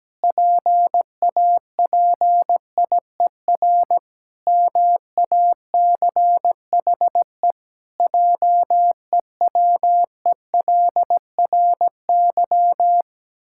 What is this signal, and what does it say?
Morse code, heard as a keyed tone, about 17 wpm, 700 Hz. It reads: PAPIER MACHE JEWELRY
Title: Pangram Challenge v2 - 17wpm